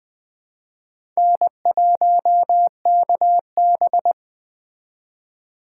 Morse code, heard as a keyed tone, about 20 wpm, 700 Hz. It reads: N1KB